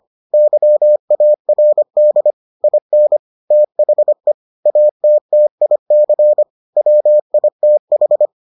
Morse code, heard as a keyed tone, about 25 wpm, 600 Hz. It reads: YARD IN THE ATTIC WITH